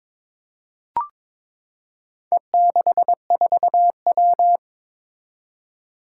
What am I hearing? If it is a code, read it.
Morse code, heard as a keyed tone, about 23 wpm, 700 Hz. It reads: E64W
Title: Four Challenging Letter-Number Combinations 22wpm